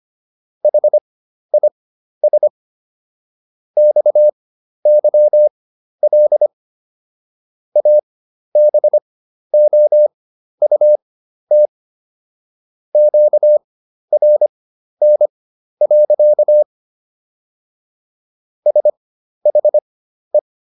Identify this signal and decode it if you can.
Morse code, heard as a keyed tone, about 25 wpm, spaced as for 12 wpm, 600 Hz. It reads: HIS XYL ABOUT QRN. SHE